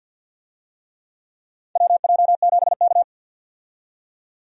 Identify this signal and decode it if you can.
Morse code, heard as a keyed tone, about 52 wpm, 700 Hz. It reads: WJ7K
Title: USA Call Signs - 4 Characters - 50wpm